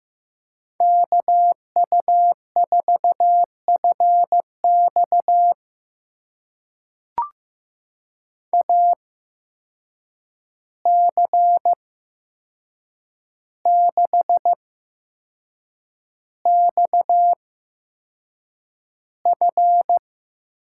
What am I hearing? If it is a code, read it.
Morse code, heard as a keyed tone, about 15 wpm, spaced as for 6 wpm, 700 Hz. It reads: KU4FX AC6XF
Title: USA Call Signs - 5 Characters - 8x Character Spacing - 15wpm